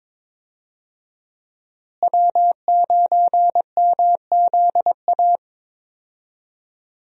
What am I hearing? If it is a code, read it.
Morse code, heard as a keyed tone, about 22 wpm, 700 Hz. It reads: W9MZA
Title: USA Call Signs - 5 Characters - Repeated 2x - 22wpm